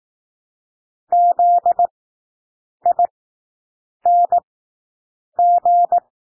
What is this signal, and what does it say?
Morse code, heard as a keyed tone, about 18 wpm, spaced as for 7 wpm, 700 Hz. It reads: ZING
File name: SOUND WORDS 18 wpm